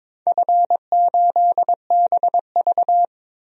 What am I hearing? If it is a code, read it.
Morse code, heard as a keyed tone, about 21 wpm, 700 Hz. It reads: F8BV
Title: DXCC-100 Call Signs - 4 Characters - 22wpm